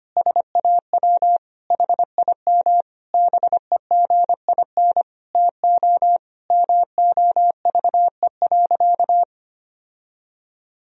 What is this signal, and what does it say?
Morse code, heard as a keyed tone, about 25 wpm, 700 Hz. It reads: SAW HIM BEGIN TO MOVE.
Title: Sentences from Top 1000 Words - Review Part 2 - 25wpm